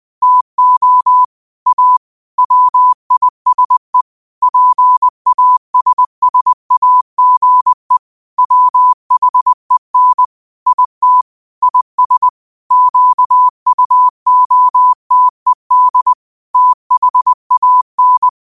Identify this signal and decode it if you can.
Morse code, heard as a keyed tone, about 20 wpm, 1000 Hz. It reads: TO A WISE PASSAGE WHEN IT IS QUOTED THAN
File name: qotdfiles / 20wpm